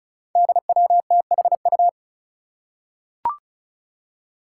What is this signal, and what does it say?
Morse code, heard as a keyed tone, about 34 wpm, 700 Hz. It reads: DWTHU
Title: Five Challenging Letter-Number Combinations 35wpm